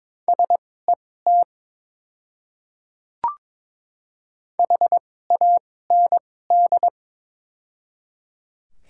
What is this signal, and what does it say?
Morse code, heard as a keyed tone, about 22 wpm, spaced as for 16 wpm, 700 Hz. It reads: SET HAND
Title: 2x Character Spacing - Top 200 Words - 22wpm